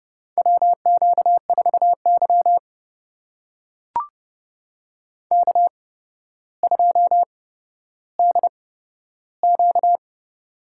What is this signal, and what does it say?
Morse code, heard as a keyed tone, about 30 wpm, spaced as for 13 wpm, 700 Hz. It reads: WQ4Y K2DQ
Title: USA Call Signs - 4 Characters - 8x Character Spacing - 30wpm